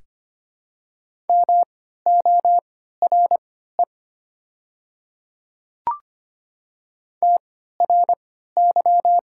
Morse code, heard as a keyed tone, about 25 wpm, spaced as for 14 wpm, 700 Hz. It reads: MORE TRY
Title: 3x Character Spacing - Top 200 Words - 25wpm